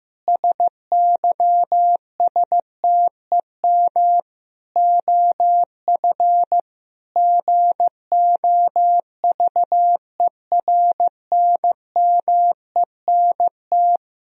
Morse code, heard as a keyed tone, about 15 wpm, 700 Hz. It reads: SYSTEM OF GOVERNMENT
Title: Sentences from Top 1000 Words - Review Part 3 - 15wpm